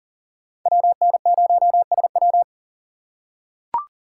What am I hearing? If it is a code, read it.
Morse code, heard as a keyed tone, about 40 wpm, 700 Hz. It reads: WN0SW
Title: USA Call Signs - 5 Characters - 4x Character Spacing - 40wpm